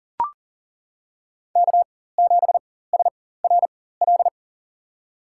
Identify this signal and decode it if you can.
Morse code, heard as a keyed tone, about 40 wpm, spaced as for 19 wpm, 700 Hz. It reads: K7SRL
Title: USA Call Signs - 5 Characters - 4x Character Spacing - 40wpm